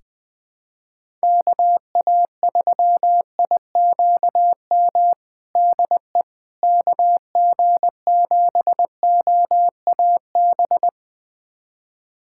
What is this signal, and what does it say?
Morse code, heard as a keyed tone, about 21 wpm, 700 Hz. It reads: KA3IQM DE KG7OAB
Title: USA Callsign DE Callsign - 6 Characters - 20wpm